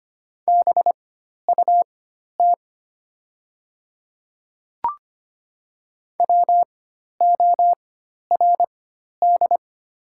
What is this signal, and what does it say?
Morse code, heard as a keyed tone, about 25 wpm, spaced as for 12 wpm, 700 Hz. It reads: BUT WORD
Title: 4x Character Spacing - Top 100 Words - 25wpm